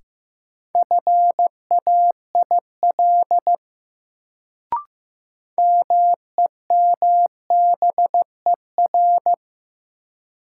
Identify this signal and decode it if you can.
Morse code, heard as a keyed tone, about 15 wpm, 700 Hz. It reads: FAIL MEMBER
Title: Sentences from Top 1100 Words - New Word Drill - 15wpm